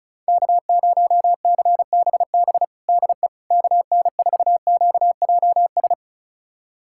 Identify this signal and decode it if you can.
Morse code, heard as a keyed tone, about 36 wpm, 700 Hz. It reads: K0CBB DE KN4QJS
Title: USA Callsign DE Callsign 35wpm